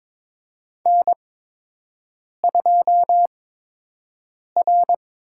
Morse code, heard as a keyed tone, about 22 wpm, spaced as for 6 wpm, 700 Hz. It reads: N2R
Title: USA Call Signs - 4 Characters - 8x Character Spacing - 22wpm